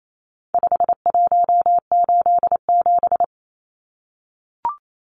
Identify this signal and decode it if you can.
Morse code, heard as a keyed tone, about 28 wpm, 700 Hz. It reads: 5187
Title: Numbers - 4-Digit 28wpm